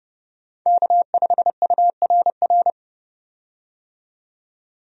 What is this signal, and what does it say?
Morse code, heard as a keyed tone, about 30 wpm, 700 Hz. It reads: K5URR
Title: USA Call Signs - 5 Characters - 30wpm